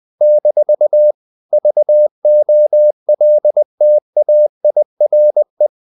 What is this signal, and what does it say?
Morse code, heard as a keyed tone, about 20 wpm, 600 Hz. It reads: - VOLTAIRE
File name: CW / Practice / EnglishQuotations0320